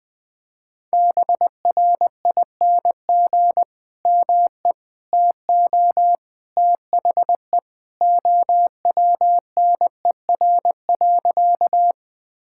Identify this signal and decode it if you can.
Morse code, heard as a keyed tone, about 20 wpm, 700 Hz. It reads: BRING ME TO THE OWNER.
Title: Sentences from Top 1600 Words 20wpm